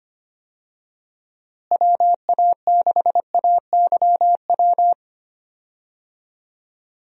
Morse code, heard as a keyed tone, about 25 wpm, 700 Hz. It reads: WA6AYW
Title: USA Call Signs - 6 Characters - Repeated 2x - 25wpm